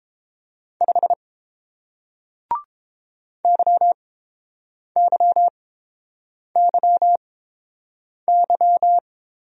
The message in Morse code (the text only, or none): 5 YYYY